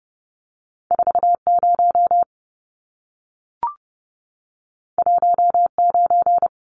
Morse code, heard as a keyed tone, about 30 wpm, 700 Hz. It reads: 40 19
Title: Numbers - 2-Digit 30wpm